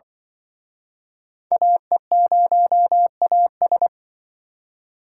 Morse code, heard as a keyed tone, about 25 wpm, 700 Hz. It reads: AE0AS
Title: Speed Racing - USA 2x2 Call Signs 20wpm